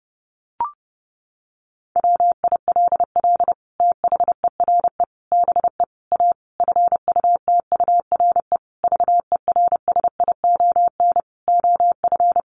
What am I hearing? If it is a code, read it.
Morse code, heard as a keyed tone, about 30 wpm, 700 Hz. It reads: WILL THERE BE A FUTURE VERSION OF
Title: Sentences from Top 1400 Words 30wpm